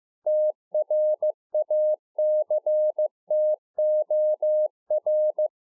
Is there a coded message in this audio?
Yes: Morse code from 0.3 to 5.5 s, heard as a keyed tone, about 15 wpm, 610 Hz. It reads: TRACTOR